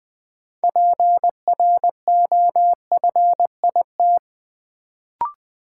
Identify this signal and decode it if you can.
Morse code, heard as a keyed tone, about 20 wpm, 700 Hz. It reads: PROFIT
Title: Sentences from Top 800 Words - New Word Drill - 20wpm